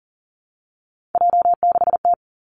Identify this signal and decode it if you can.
Morse code, heard as a keyed tone, about 38 wpm, 690 Hz. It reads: J6T
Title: Three Challenging Letter-Number Combinations 40wpm